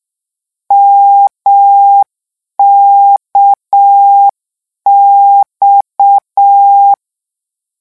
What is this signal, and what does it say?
Morse code, heard as a keyed tone, about 6 wpm, 800 Hz. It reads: MKX
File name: lecon 1-vitesse 6